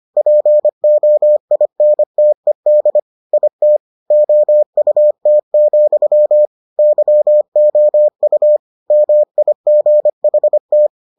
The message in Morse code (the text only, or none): POINTED IT OUT, YOU MIGHT